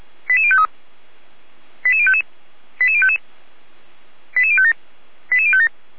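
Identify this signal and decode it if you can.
Five-tone selective call (ZVEI), heard as quick runs of five tones, tones of about 70 ms. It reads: 88053 88055 88055 88057 88057
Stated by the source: Five-tone ZVEI selective call received with an SDR, Feuerwehr Orlamünde.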